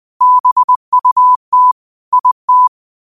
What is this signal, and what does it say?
Morse code, heard as a keyed tone, about 20 wpm, 1000 Hz. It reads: BUT IT
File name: qotdfiles / 20wpm